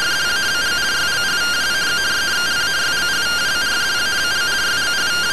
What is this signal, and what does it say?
Two-tone teleprinter signal, about 170 Hz shift.
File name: findings / CT AXIAL -256.gen-10 rtty rx